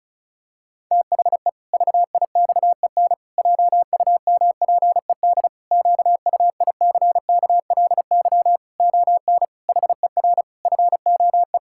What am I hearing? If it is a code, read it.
Morse code, heard as a keyed tone, about 34 wpm, 700 Hz. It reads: THE VIXEN JUMPED QUICKLY ON HER FOE
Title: Pangram Challenge v2 - 35wpm